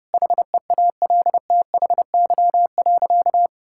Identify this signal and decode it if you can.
Morse code, heard as a keyed tone, about 32 wpm, 700 Hz. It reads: HEALTHY.
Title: Sentences from Top 1000 Words - Review Part 5 - 30wpm